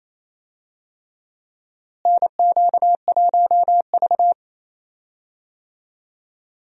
Morse code, heard as a keyed tone, about 28 wpm, 700 Hz. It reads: NQ1V
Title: USA Call Signs - 4 Characters - Repeated 3x - 28wpm